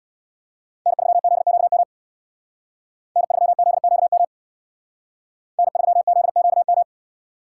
Morse code, heard as a keyed tone, about 63 wpm, 700 Hz. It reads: N3ZQM N3ZQM N3ZQM